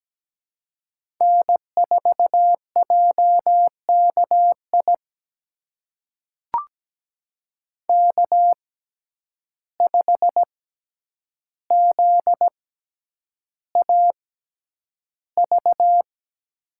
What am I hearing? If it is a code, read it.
Morse code, heard as a keyed tone, about 17 wpm, spaced as for 9 wpm, 700 Hz. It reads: N4JKI K5ZAV
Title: USA Call Signs - 5 Characters - 6x Character Spacing - 17wpm